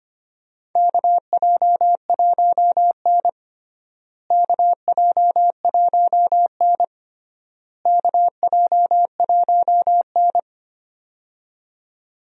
Morse code, heard as a keyed tone, about 25 wpm, 700 Hz. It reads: KJ1N KJ1N KJ1N
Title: USA Call Signs - 4 Characters - Repeated 3x - 25wpm